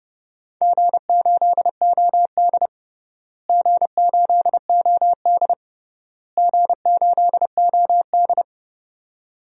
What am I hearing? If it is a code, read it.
Morse code, heard as a keyed tone, about 28 wpm, 700 Hz. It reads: G8OD G8OD G8OD
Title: DXCC-100 Call Signs - 4 Characters - Repeated 3x - 30wpm